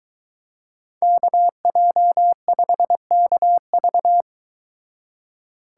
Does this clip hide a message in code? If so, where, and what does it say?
Yes: Morse code from 1.0 to 4.2 s, heard as a keyed tone, about 22 wpm, 700 Hz. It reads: KJ5KV